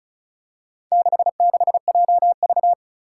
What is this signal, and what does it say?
Morse code, heard as a keyed tone, about 36 wpm, 700 Hz. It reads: 66JV
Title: Four Challenging Letter-Number Combinations 35wpm